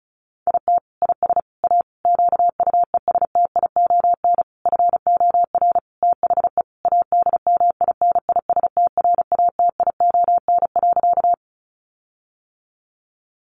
Morse code, heard as a keyed tone, about 34 wpm, 700 Hz. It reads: IT IS A QUESTION FOR THE ADMINISTRATION.
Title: Sentences from Top 1300 Words 35wpm